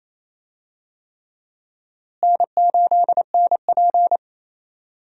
Morse code, heard as a keyed tone, about 28 wpm, 700 Hz. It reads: N8NP
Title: USA Call Signs - 4 Characters - Repeated 3x - 28wpm